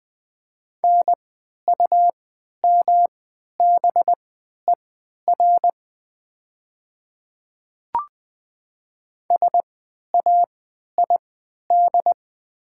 Morse code, heard as a keyed tone, about 20 wpm, spaced as for 11 wpm, 700 Hz. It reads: NUMBER SAID
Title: 3x Character Spacing - Top 100 Words - 20wpm